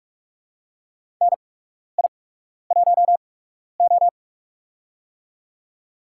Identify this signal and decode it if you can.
Morse code, heard as a keyed tone, about 44 wpm, spaced as for 12 wpm, 690 Hz. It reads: NI1O
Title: USA Call Signs - 4 Characters - 8x Character Spacing - 45wpm